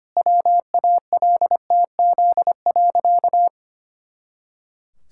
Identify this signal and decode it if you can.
Morse code, heard as a keyed tone, about 25 wpm, 700 Hz. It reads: WALTZ.
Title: Pangram Challenge v2 - 25wpm